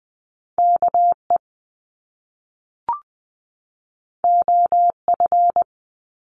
Morse code, heard as a keyed tone, about 20 wpm, 700 Hz. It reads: KE OF
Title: Top Two Letter Combinations in Words 20wpm